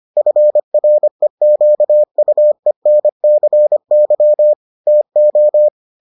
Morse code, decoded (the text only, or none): FREQUENCY TO